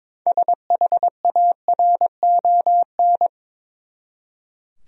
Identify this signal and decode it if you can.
Morse code, heard as a keyed tone, about 22 wpm, 700 Hz. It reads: SHARON
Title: USA Common Names - 22wpm